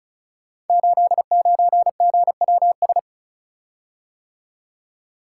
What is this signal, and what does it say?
Morse code, heard as a keyed tone, about 36 wpm, 700 Hz. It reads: 89GWS
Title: Code Groups - 5 Letter:Numbers - 35wpm